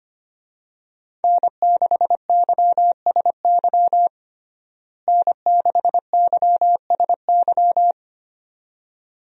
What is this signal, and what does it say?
Morse code, heard as a keyed tone, about 25 wpm, 700 Hz. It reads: N6YSY N6YSY
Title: USA Call Signs - 5 Characters - Repeated 2x - 25wpm